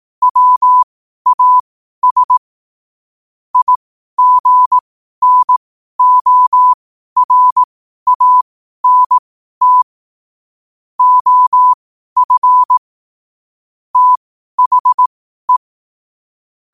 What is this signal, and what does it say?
Morse code, heard as a keyed tone, about 18 wpm, spaced as for 12 wpm, 1000 Hz. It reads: WAS IGNORANT OF THE
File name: qotdfiles / 12wpm